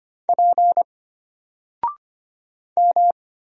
Morse code, heard as a keyed tone, about 25 wpm, 700 Hz. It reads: P M
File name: Single-Letter-Number-25wpm